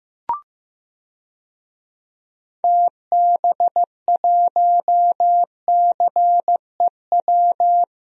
Morse code, heard as a keyed tone, about 16 wpm, 700 Hz. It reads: TB1CEW
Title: DXCC-100 Call Signs - 6 Characters - Repeated 2x - 15wpm